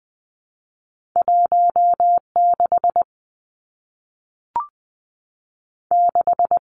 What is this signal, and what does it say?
Morse code, heard as a keyed tone, about 20 wpm, 700 Hz. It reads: 1 6 6
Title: Two Challenging Letter-Number Combinations 20wpm